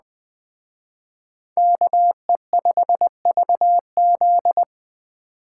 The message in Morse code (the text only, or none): KE5VZ